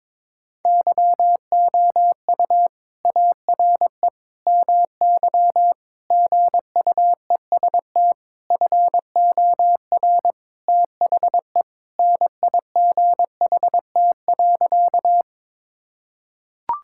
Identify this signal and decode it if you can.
Morse code, heard as a keyed tone, about 21 wpm, 700 Hz. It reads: YOU ARE MY GUEST FOR THE NIGHT.
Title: Sentences from Top 1600 Words 22wpm